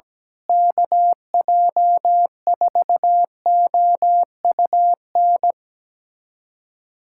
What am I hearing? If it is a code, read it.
Morse code, heard as a keyed tone, about 17 wpm, 700 Hz. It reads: KJ4OUN
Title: USA Callsign DE Callsign - 6 Characters - 17wpm